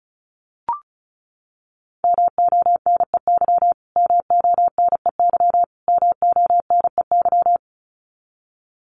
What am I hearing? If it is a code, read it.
Morse code, heard as a keyed tone, about 34 wpm, 700 Hz. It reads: MONEY MONEY MONEY
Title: Top 500 Words Repeated 35wpm